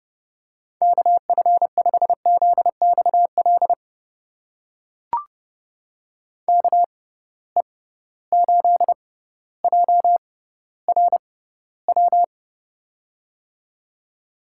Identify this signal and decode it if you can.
Morse code, heard as a keyed tone, about 30 wpm, spaced as for 15 wpm, 700 Hz. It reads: KF5ZXL KE8JRW
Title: USA Call Signs - 6 Characters - 6x Character Spacing - 30wpm